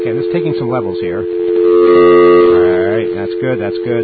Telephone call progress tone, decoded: dial tone